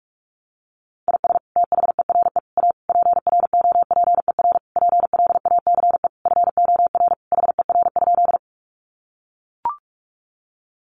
Morse code, heard as a keyed tone, about 44 wpm, 700 Hz. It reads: IS THERE A PROPER PLACE FOR HER?
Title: Sentences from Top 800 Words 45wpm